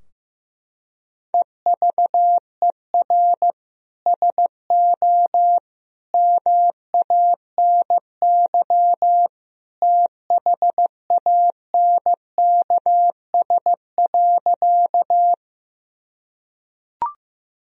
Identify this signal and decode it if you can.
Morse code, heard as a keyed tone, about 15 wpm, 700 Hz. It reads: EVER SO MANY THANKS.
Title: Sentences from Top 1000 Words - Review Part 3 - 15wpm